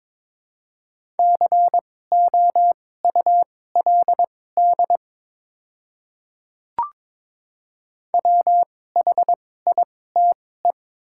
Morse code, heard as a keyed tone, about 22 wpm, spaced as for 16 wpm, 700 Hz. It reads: COULD WHITE
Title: 2x Character Spacing - Top 300 Words - 22wpm